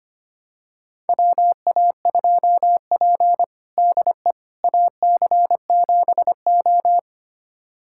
Morse code, heard as a keyed tone, about 25 wpm, 700 Hz. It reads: WA2P DE AC7O
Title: USA Callsign DE Callsign - 4 Characters - 25wpm